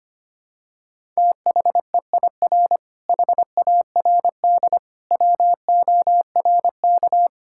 Morse code, heard as a keyed tone, about 25 wpm, 700 Hz. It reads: THEIR HARD WORK
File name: practice / 25wpm / Difficult Sentences with Hints - 25wpm